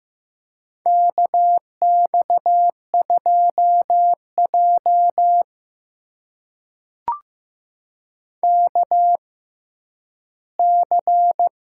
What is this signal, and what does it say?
Morse code, heard as a keyed tone, about 15 wpm, spaced as for 10 wpm, 700 Hz. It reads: KX2J KC
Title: USA Call Signs - 6x Character Spacing - 15wpm